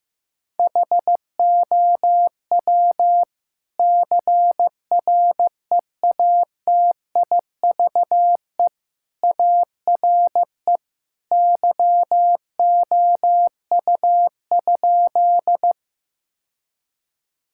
Morse code, heard as a keyed tone, about 15 wpm, 700 Hz. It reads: HOW CREATIVE ARE YOU?
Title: Sentences from Top 1000 Words - Review Part 5 - 15wpm